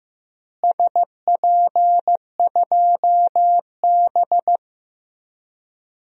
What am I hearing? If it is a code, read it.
Morse code, heard as a keyed tone, about 15 wpm, 700 Hz. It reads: SP2B
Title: DXCC-100 Call Signs - 4 Characters - Repeated 2x - 15wpm